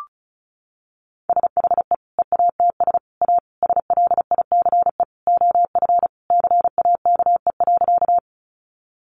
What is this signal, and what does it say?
Morse code, heard as a keyed tone, about 34 wpm, 700 Hz. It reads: SHE EATS A SLICE OF CAKE.
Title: Easy Phrases 35wpm